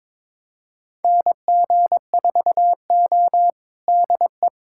Morse code, heard as a keyed tone, about 22 wpm, 700 Hz. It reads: NG4O DE
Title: USA Callsign DE Callsign - 4 Characters - 22wpm